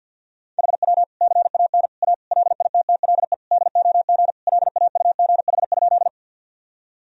Morse code, heard as a keyed tone, about 52 wpm, 700 Hz. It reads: HW CAN A LITTLE DOG LAUGH?
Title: Sentences from Top 500 Words and Abbreviations 50wpm